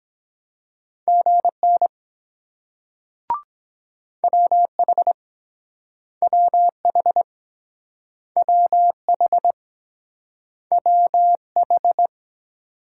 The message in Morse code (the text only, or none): GN WH WH WH WH